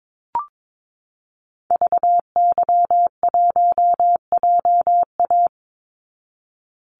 Morse code, heard as a keyed tone, about 22 wpm, 700 Hz. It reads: VY1JA
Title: Call Signs Repeated 3x - 22wpm